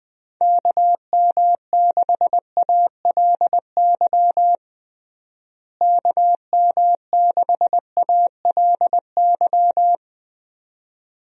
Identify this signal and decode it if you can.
Morse code, heard as a keyed tone, about 20 wpm, 700 Hz. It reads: KM6ALY KM6ALY
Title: USA Call Signs - 6 Characters - Repeated 2x - 20wpm